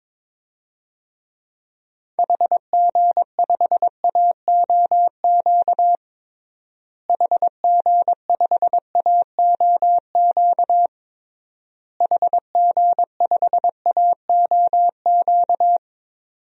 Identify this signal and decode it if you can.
Morse code, heard as a keyed tone, about 22 wpm, 700 Hz. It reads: HG5AOQ HG5AOQ HG5AOQ
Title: DXCC-100 Call Signs - 6 Characters - Repeated 3x - 22wpm